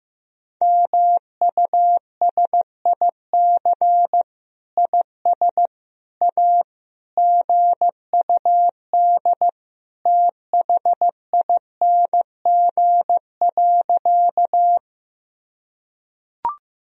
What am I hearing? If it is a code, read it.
Morse code, heard as a keyed tone, about 15 wpm, 700 Hz. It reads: MUSIC IS A GUD THING.
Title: Sentences from Top 300 Words and Abbreviations 15wpm